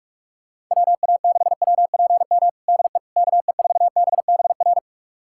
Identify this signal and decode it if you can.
Morse code, heard as a keyed tone, about 46 wpm, 700 Hz. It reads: WA6WPM DE KE4BBR
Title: USA Callsign DE Callsign - 6 Characters - 45wpm